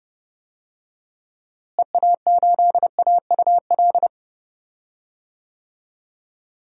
Morse code, heard as a keyed tone, about 32 wpm, 700 Hz. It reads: EA8AUL
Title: DXCC-100 Call Signs - 6 Characters - Repeated 3x - 30wpm